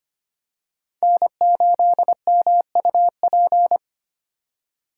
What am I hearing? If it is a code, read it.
Morse code, heard as a keyed tone, about 25 wpm, 700 Hz. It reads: N8MUP
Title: USA Call Signs - 5 Characters - 2x Character Spacing - 25wpm